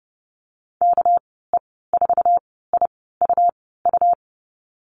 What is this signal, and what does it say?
Morse code, heard as a keyed tone, about 30 wpm, spaced as for 17 wpm, 700 Hz. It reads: KE4IUU
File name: Lesson 37g - Letter x - Call Signs 30wpm